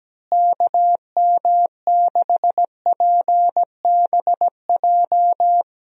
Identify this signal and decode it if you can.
Morse code, heard as a keyed tone, about 17 wpm, 700 Hz. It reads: KM6PBJ
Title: USA Call Signs - 6 Characters - Repeated 2x - 17wpm